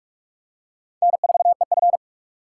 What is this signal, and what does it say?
Morse code, heard as a keyed tone, about 46 wpm, 700 Hz. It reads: N4EF
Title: USA Call Signs - 4 Characters - 3x Character Spacing - 45wpm